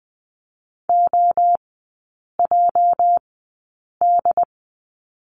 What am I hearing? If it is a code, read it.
Morse code, heard as a keyed tone, about 20 wpm, spaced as for 8 wpm, 700 Hz. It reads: OJD
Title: 218 Lesson 35e - Number 0 - Three Characters 20wpm